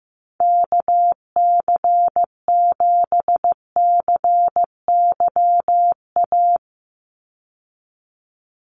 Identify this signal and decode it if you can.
Morse code, heard as a keyed tone, about 16 wpm, 700 Hz. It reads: KC7CYA
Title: USA Call Signs 15wpm